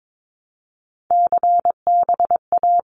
Morse code, heard as a keyed tone, about 22 wpm, 700 Hz. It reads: CBA
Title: Abbreviations 22wpm